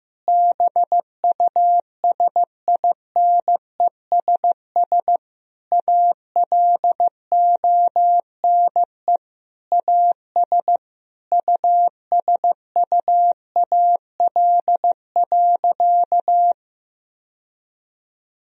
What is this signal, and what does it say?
Morse code, heard as a keyed tone, about 15 wpm, 700 Hz. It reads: BUSINESS ALONE AS USUAL.
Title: Sentences from Top 1000 Words - Review Part 2 - 15wpm